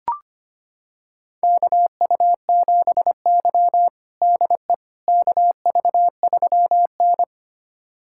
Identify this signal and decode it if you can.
Morse code, heard as a keyed tone, about 25 wpm, 700 Hz. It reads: KU7Y DE KV3N
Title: USA Callsign DE Callsign - 4 Characters - 25wpm